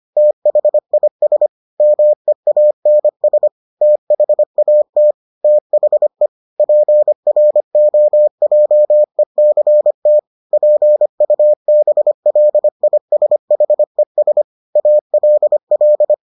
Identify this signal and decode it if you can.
Morse code, heard as a keyed tone, about 25 wpm, 600 Hz. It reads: THIS MEANS THAT THE PROJECT PUBLISHES ALL